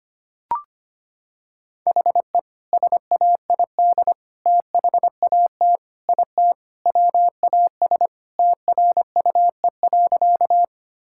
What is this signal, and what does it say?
Morse code, heard as a keyed tone, about 25 wpm, 690 Hz. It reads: HE SAID THAT IT WAS TRUE.
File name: practice / 25wpm / Sentences with 2-Word Phrases - 25wpm